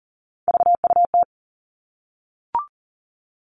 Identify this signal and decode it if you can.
Morse code, heard as a keyed tone, about 40 wpm, 700 Hz. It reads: VUT
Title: All Cut Numbers - 3-Digit 40wpm